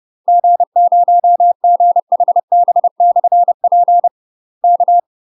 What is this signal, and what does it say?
Morse code, heard as a keyed tone, about 30 wpm, 700 Hz. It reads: G0GHB/P K
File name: UK MASTER QSOs TRAINING/30WPM